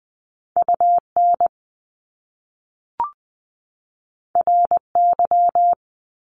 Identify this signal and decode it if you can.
Morse code, heard as a keyed tone, about 20 wpm, 700 Hz. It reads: UN RY
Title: Top Two Letter Combinations in Words 20wpm